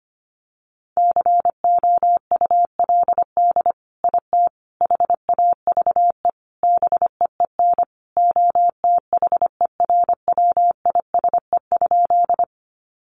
Morse code, heard as a keyed tone, about 25 wpm, 700 Hz. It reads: COULD IT HAVE BEEN OTHERWISE?
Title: Sentences from Top 1200 Words 25wpm